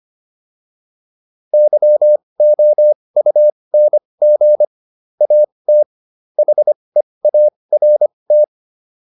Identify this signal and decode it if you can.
Morse code, heard as a keyed tone, about 25 wpm, spaced as for 20 wpm, 600 Hz. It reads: YOUNG AT HEART